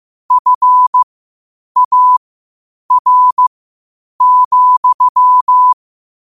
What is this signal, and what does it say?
Morse code, heard as a keyed tone, about 15 wpm, spaced as for 8 wpm, 1000 Hz. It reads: FAR,
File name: qotdfiles / 8wpm